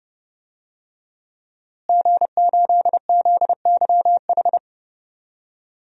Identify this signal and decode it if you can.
Morse code, heard as a keyed tone, about 30 wpm, 700 Hz. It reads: G8ZYH
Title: DXCC-100 Call Signs - 5 Characters - Repeated 3x - 30wpm